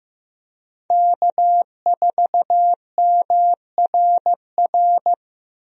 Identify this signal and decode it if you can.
Morse code, heard as a keyed tone, about 15 wpm, 700 Hz. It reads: K4MRR
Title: USA Call Signs - 5 Characters - Repeated 3x - 15wpm